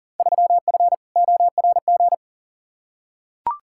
Morse code, heard as a keyed tone, about 40 wpm, 700 Hz. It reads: 3F ORG
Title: Field Day Exchanges - 40wpm